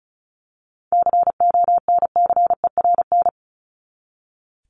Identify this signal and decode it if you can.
Morse code, heard as a keyed tone, about 34 wpm, 700 Hz. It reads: CONCERN
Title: Top 751-1000 Words 35wpm